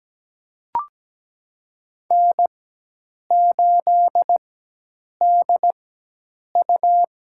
Morse code, heard as a keyed tone, about 17 wpm, spaced as for 8 wpm, 700 Hz. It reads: N8DU
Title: USA Call Signs - 4x Character Spacing - 17wpm